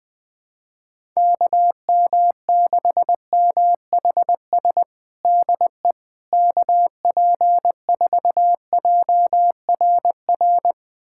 Morse code, heard as a keyed tone, about 21 wpm, 700 Hz. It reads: KM6MHS DE KP4JRR
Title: USA Callsign DE Callsign - 6 Characters - 20wpm